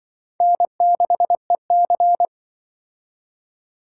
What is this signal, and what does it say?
Morse code, heard as a keyed tone, about 25 wpm, 700 Hz. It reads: N6EC
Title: Speed Racing - USA 1x2 Call Signs 20wpm